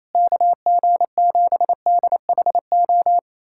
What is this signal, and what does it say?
Morse code, heard as a keyed tone, about 28 wpm, 700 Hz. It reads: KG7DHO
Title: USA Callsign DE Callsign - 6 Characters - 28wpm